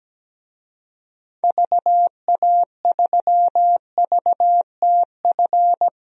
Morse code, heard as a keyed tone, about 17 wpm, 700 Hz. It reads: VA3VTF
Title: DXCC-100 Call Signs - 6 Characters - Repeated 2x - 17wpm